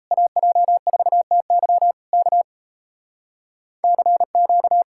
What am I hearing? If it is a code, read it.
Morse code, heard as a keyed tone, about 38 wpm, 700 Hz. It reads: AJ4TY K CQ